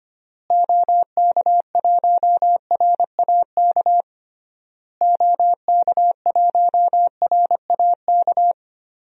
Morse code, heard as a keyed tone, about 25 wpm, 700 Hz. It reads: OK1RAK OK1RAK
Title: DXCC-100 Call Signs - 6 Characters - Repeated 3x - 25wpm